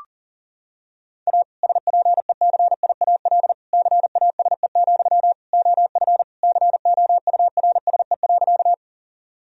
Morse code, heard as a keyed tone, about 40 wpm, 700 Hz. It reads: A SPECIAL CASE, OF COURSE.
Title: Sentences from Top 1000 Words - Review Part 3 - 40wpm